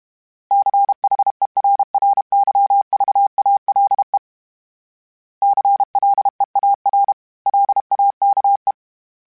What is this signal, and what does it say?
Morse code, heard as a keyed tone, about 32 wpm, 800 Hz. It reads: CHERRYVALE CLEAR LAKE
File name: fromSMs / FromSMs mp3s / Cities@30